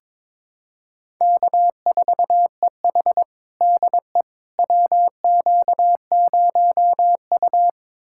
Morse code, heard as a keyed tone, about 22 wpm, 700 Hz. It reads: K4EH DE WQ0U
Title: USA Callsign DE Callsign - 4 Characters - 22wpm